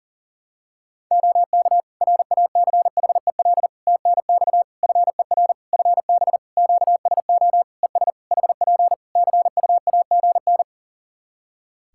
Morse code, heard as a keyed tone, about 40 wpm, 700 Hz. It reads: OK RACHEL TNX FER FB QSO ES HP CUAGN
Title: K1USN SST QSO Practice 40wpm